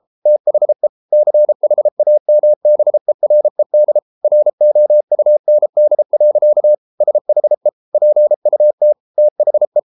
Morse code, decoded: THE CHAMBERED ROUND. SHE PUT THE

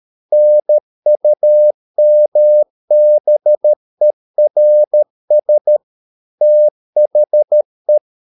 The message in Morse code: NUMBERS THE

